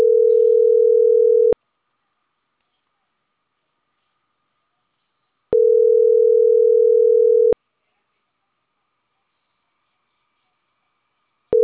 Ringback tone (North America), 440 + 480 Hz.